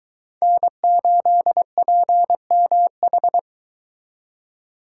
Morse code, heard as a keyed tone, about 22 wpm, 700 Hz. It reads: N8PMH